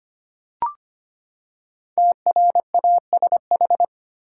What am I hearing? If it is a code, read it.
Morse code, heard as a keyed tone, about 25 wpm, 700 Hz. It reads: TRASH
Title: Five Letter Words - 25wpm